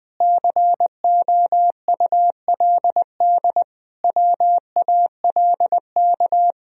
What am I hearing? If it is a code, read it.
Morse code, heard as a keyed tone, about 21 wpm, 700 Hz. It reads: COULD WALK
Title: Sentences from Top 1000 Words - Review Part 1 - 20wpm